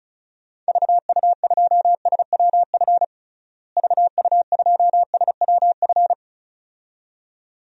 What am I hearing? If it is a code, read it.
Morse code, heard as a keyed tone, about 36 wpm, 700 Hz. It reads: VU2SWF VU2SWF
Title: DXCC-100 Call Signs - 6 Characters - Repeated 2x - 35wpm